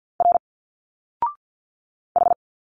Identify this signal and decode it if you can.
Morse code, heard as a keyed tone, about 51 wpm, 700 Hz. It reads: R H
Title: Single Letters 50wpm